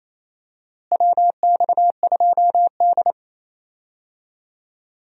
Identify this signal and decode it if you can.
Morse code, heard as a keyed tone, about 28 wpm, 700 Hz. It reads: WX2D
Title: USA Call Signs - 4 Characters - Repeated 2x - 28wpm